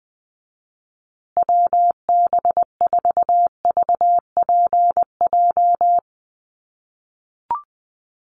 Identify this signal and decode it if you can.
Morse code, heard as a keyed tone, about 20 wpm, 700 Hz. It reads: WB4VPJ
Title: USA Call Signs Repeated 2x 20wpm